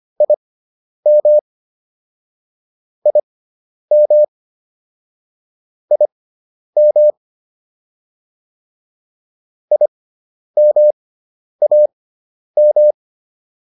Morse code, heard as a keyed tone, about 25 wpm, spaced as for 10 wpm, 600 Hz. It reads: IM IM IM IMAM
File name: IM4 10